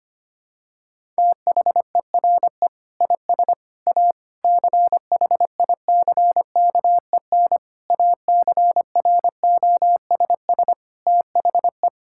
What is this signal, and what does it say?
Morse code, heard as a keyed tone, about 25 wpm, 700 Hz. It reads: THERE IS A CHICKEN ACROSS THE
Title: Sentences from Top 1000 Words - Review Part 5 - 25wpm